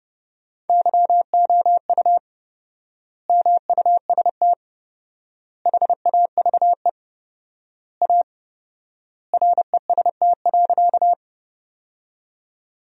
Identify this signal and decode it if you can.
Morse code, heard as a keyed tone, about 30 wpm, 700 Hz. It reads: YOU MUST HAVE A REST.
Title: 4x Word-Spacing - Sentences from Top 500 Words - 30wpm